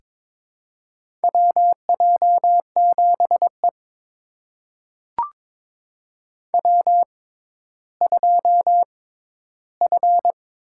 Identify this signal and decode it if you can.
Morse code, heard as a keyed tone, about 22 wpm, spaced as for 12 wpm, 700 Hz. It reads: WJ7E W2F
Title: USA Call Signs - 4 Characters - 6x Character Spacing - 22wpm